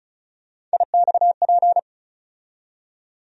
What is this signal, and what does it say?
Morse code, heard as a keyed tone, about 36 wpm, 700 Hz. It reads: IXP
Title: USA Call Sign Postfixes - 3 Letters 35wpm